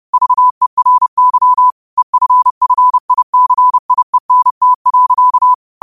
Morse code, heard as a keyed tone, about 30 wpm, 1000 Hz. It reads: VERY EFFICIENT.
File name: qotdfiles / 30wpm